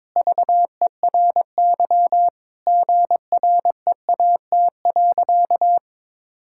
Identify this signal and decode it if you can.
Morse code, heard as a keyed tone, about 22 wpm, 700 Hz. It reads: VERY GREAT.
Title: Sentences from Top 1000 Words - Review Part 5 - 22wpm